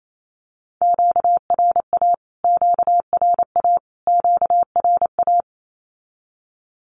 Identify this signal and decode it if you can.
Morse code, heard as a keyed tone, about 28 wpm, 700 Hz. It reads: QRA QRA QRA
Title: Q-Codes Repeated 3x 28wpm